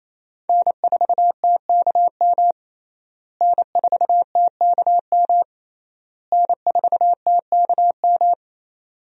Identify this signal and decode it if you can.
Morse code, heard as a keyed tone, about 28 wpm, 700 Hz. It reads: N4TKM N4TKM N4TKM
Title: USA Call Signs - 5 Characters - Repeated 3x - 28wpm